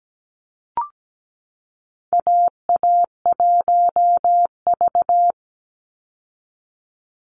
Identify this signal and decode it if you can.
Morse code, heard as a keyed tone, about 17 wpm, 700 Hz. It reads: AA1V
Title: Call Signs Repeated 3x - 17wpm